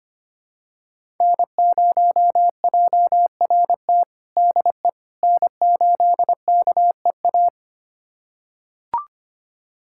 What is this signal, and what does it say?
Morse code, heard as a keyed tone, about 25 wpm, 700 Hz. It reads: N0JRT DE N8KEA
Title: USA Callsign DE Callsign - 5 Characters - 25wpm